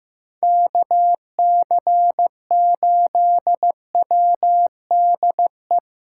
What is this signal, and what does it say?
Morse code, heard as a keyed tone, about 17 wpm, 700 Hz. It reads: KC8WDE
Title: Speed Racing - USA 2x3 Call Signs 15wpm